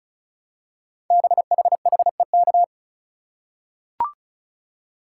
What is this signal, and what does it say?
Morse code, heard as a keyed tone, about 36 wpm, 690 Hz. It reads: BHHEK
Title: Code Groups - 5 Letters - 35wpm